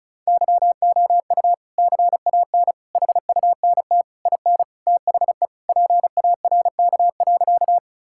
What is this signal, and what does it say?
Morse code, heard as a keyed tone, about 34 wpm, 700 Hz. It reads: YOU CAN HUNT IN THE PARK.
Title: Sentences from Top 1600 Words 35wpm